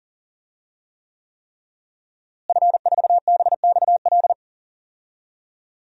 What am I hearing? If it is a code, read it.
Morse code, heard as a keyed tone, about 40 wpm, 700 Hz. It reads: F4BXL
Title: DXCC-100 Call Signs - 5 Characters - Repeated 3x - 40wpm